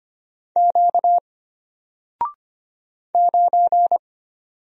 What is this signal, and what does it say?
Morse code, heard as a keyed tone, about 25 wpm, 700 Hz. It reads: Q 9
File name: Single-Letter-Number-25wpm